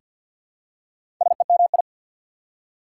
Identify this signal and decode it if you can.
Morse code, heard as a keyed tone, about 53 wpm, 690 Hz. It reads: SEMI
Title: Prefixes and Suffixes 50wpm